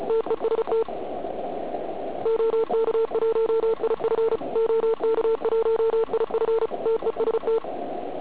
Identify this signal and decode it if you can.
Morse code, heard as a keyed tone, about 33 wpm, 440 Hz, with noise. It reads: TEST OK1IF OK1IF TEST
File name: Archiv / CUC106 / cuc106 PILEUP